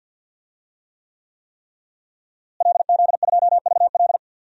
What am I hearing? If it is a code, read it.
Morse code, heard as a keyed tone, about 51 wpm, 700 Hz. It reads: LZ2VL
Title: DXCC-100 Call Signs - 5 Characters - Repeated 3x - 50wpm